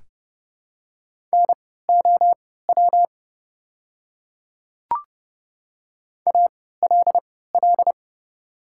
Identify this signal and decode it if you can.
Morse code, heard as a keyed tone, about 30 wpm, spaced as for 17 wpm, 690 Hz. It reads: NOW ALL
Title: 3x Character Spacing - Top 500 Words - 30wpm